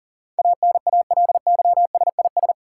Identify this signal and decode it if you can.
Morse code, heard as a keyed tone, about 40 wpm, 700 Hz. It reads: ANALYSIS
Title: Sentences from Top 1400 Words - New Word Drill Alt - 40wpm